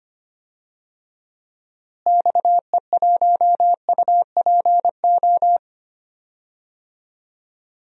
Morse code, heard as a keyed tone, about 25 wpm, 700 Hz. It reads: XE1UPO
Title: DXCC-100 Call Signs - 6 Characters - Repeated 3x - 25wpm